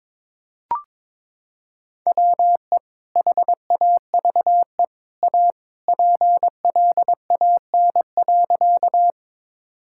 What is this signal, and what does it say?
Morse code, heard as a keyed tone, about 22 wpm, 700 Hz. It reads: WE HAVE A PLAN.
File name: practice / 22wpm / Sentences with 2-Word Phrases - 22wpm